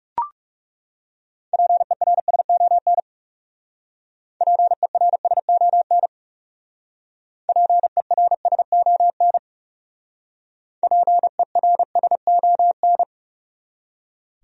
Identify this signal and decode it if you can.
Morse code, heard as a keyed tone, about 34 wpm, 700 Hz. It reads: PERSON PERSON PERSON PERSON
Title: Speed-Racing Top 500 Words 30wpm